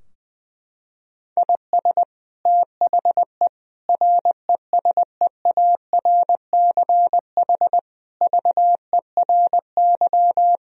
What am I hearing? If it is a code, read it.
Morse code, heard as a keyed tone, about 20 wpm, 700 Hz. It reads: IS THE RESEARCH VERY
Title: Sentences from Top 1000 Words - Review Part 4 - 20wpm